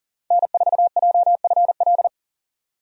Morse code, heard as a keyed tone, about 40 wpm, 700 Hz. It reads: N4JFL
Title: USA Call Signs - 5 Characters - 40wpm